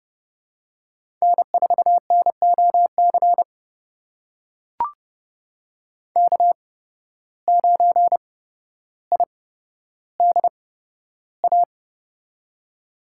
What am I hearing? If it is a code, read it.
Morse code, heard as a keyed tone, about 30 wpm, spaced as for 13 wpm, 700 Hz. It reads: N4NOC K9IDA
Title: USA Call Signs - 5 Characters - 8x Character Spacing - 30wpm